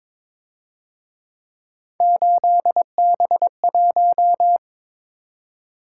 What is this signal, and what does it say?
Morse code, heard as a keyed tone, about 26 wpm, 700 Hz. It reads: 8B1
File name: Speed-Racing - Three Letter-Number Combinations - 22wpm